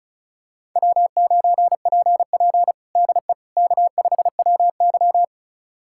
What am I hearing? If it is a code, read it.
Morse code, heard as a keyed tone, about 36 wpm, 700 Hz. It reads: W9PP DE K5WY
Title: USA Callsign DE Callsign - 4 Characters - 35wpm